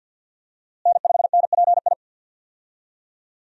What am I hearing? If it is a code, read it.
Morse code, heard as a keyed tone, about 52 wpm, 700 Hz. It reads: N5NPI
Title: USA Call Signs - 5 Characters - Repeated 2x - 50wpm